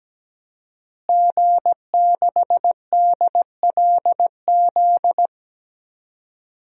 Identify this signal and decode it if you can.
Morse code, heard as a keyed tone, about 17 wpm, 700 Hz. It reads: G6DLZ
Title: DXCC-100 Call Signs - 5 Characters - Repeated 3x - 17wpm